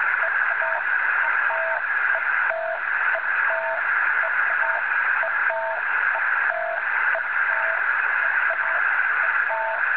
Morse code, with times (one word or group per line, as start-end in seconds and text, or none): none